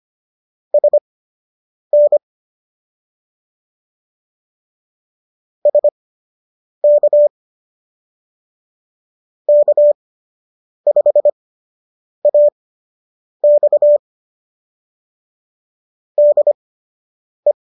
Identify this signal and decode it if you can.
Morse code, heard as a keyed tone, about 25 wpm, spaced as for 8 wpm, 590 Hz. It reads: SN SK K5AX DE